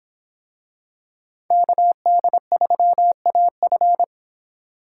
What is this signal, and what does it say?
Morse code, heard as a keyed tone, about 25 wpm, 700 Hz. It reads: KD3AF